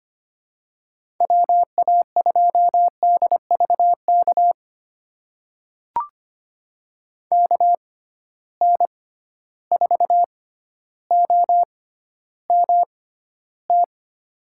Morse code, heard as a keyed tone, about 25 wpm, spaced as for 13 wpm, 700 Hz. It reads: WA2DVK KN4OMT
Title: USA Call Signs - 6 Characters - 6x Character Spacing - 25wpm